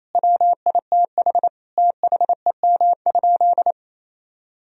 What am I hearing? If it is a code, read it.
Morse code, heard as a keyed tone, about 28 wpm, 700 Hz. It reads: WITH THEM?
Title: Sentences from Top 1000 Words - Review Part 1 - 28wpm